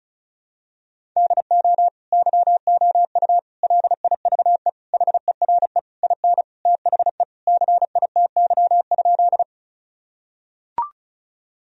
Morse code, heard as a keyed tone, about 36 wpm, 700 Hz. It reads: DO YOU LIVE HERE IN THE CITY?
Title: Sentences from Top 1000 Words - Review Part 1 - 35wpm